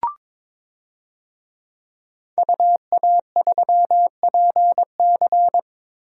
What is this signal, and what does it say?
Morse code, heard as a keyed tone, about 22 wpm, 700 Hz. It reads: UA3PC
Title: DXCC-100 Call Signs - 5 Characters - Repeated 3x - 22wpm